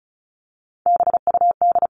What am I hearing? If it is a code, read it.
Morse code, heard as a keyed tone, about 34 wpm, 700 Hz. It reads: BUD
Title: CWT Exchange 35wpm